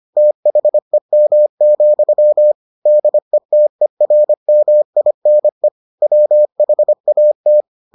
Morse code, heard as a keyed tone, about 25 wpm, 600 Hz. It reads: THEM, DETERMINE WHAT